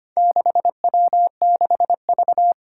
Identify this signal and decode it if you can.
Morse code, heard as a keyed tone, about 25 wpm, 700 Hz. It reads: 6W6V
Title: Four Challenging Letter-Number Combinations 25wpm